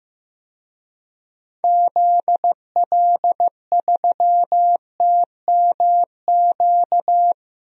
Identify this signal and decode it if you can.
Morse code, heard as a keyed tone, about 15 wpm, 700 Hz. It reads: ZL3TMQ
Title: DXCC-100 Call Signs - 6 Characters - 15wpm